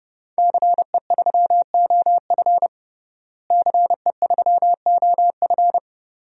Morse code, heard as a keyed tone, about 30 wpm, 700 Hz. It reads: CE3OF CE3OF
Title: DXCC-100 Call Signs - 5 Characters - Repeated 2x - 30wpm